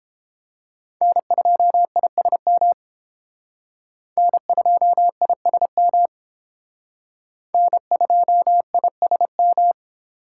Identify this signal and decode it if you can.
Morse code, heard as a keyed tone, about 32 wpm, 700 Hz. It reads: N2ISM N2ISM N2ISM